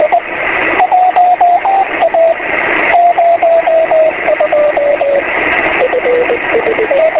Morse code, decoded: none